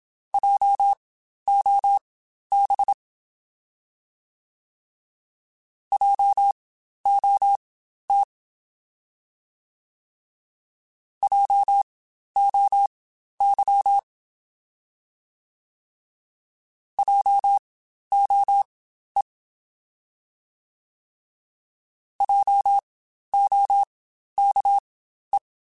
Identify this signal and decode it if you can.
Morse code, heard as a keyed tone, about 27 wpm, spaced as for 12 wpm, 800 Hz. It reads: JOB JOT JOY JOE JOKE
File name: Spaced Alphas / Spaced Alphas mp3s / Js sp @25